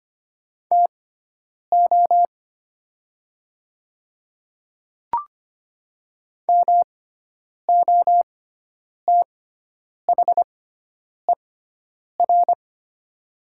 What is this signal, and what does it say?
Morse code, heard as a keyed tone, about 25 wpm, spaced as for 9 wpm, 700 Hz. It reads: TO MOTHER
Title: 6x Character Spacing - Top 200 Words - 25wpm